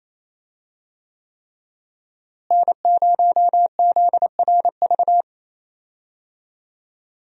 Morse code, heard as a keyed tone, about 28 wpm, 700 Hz. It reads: N0ZRV
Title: USA Call Signs - 5 Characters - Repeated 3x - 28wpm